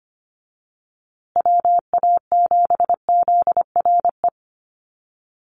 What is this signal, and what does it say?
Morse code, heard as a keyed tone, about 25 wpm, 700 Hz. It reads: WA7ZRE
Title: USA Call Signs Repeated 2x 25wpm